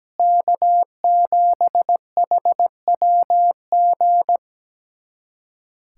Morse code, heard as a keyed tone, about 17 wpm, 700 Hz. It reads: K7HWG
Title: USA Callsign DE Callsign - 5 Characters - 17wpm